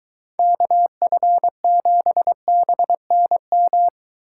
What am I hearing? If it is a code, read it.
Morse code, heard as a keyed tone, about 23 wpm, 700 Hz. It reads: KF7BNM